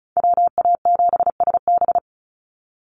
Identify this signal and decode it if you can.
Morse code, heard as a keyed tone, about 34 wpm, 700 Hz. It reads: WA7SB
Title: NA SOTA Chaser Callsigns 35wpm